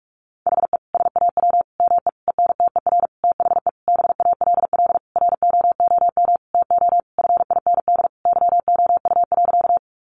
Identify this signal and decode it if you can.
Morse code, heard as a keyed tone, about 44 wpm, 700 Hz. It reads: HE SAW ME ENTER THE BALL ROOM TO FIND YOU.
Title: Sentences from Top 1300 Words 45wpm